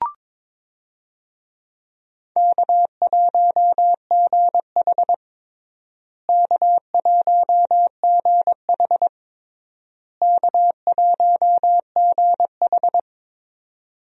Morse code, heard as a keyed tone, about 22 wpm, 700 Hz. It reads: K1GH K1GH K1GH
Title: USA Call Signs - 4 Characters - Repeated 3x - 22wpm